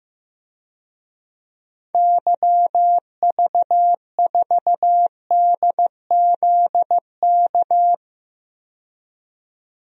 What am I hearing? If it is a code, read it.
Morse code, heard as a keyed tone, about 15 wpm, 700 Hz. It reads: YV4DZK
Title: DXCC-100 Call Signs - 6 Characters - 15wpm